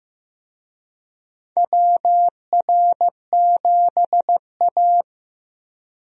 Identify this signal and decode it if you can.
Morse code, heard as a keyed tone, about 15 wpm, 700 Hz. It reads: WR7A
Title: USA Call Signs - 4 Characters - Repeated 2x - 15wpm